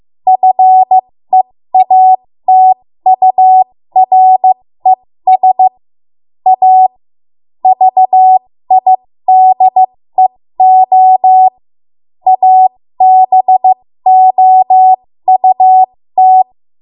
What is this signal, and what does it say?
Morse code, heard as a keyed tone, about 15 wpm, spaced as for 13 wpm, 750 Hz. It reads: FEATURES A VIDEO ABOUT
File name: Morse / 13WPM